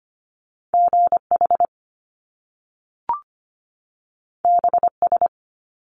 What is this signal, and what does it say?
Morse code, heard as a keyed tone, about 25 wpm, 700 Hz. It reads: GH BS
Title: Top Two Letter Combinations in Words 25wpm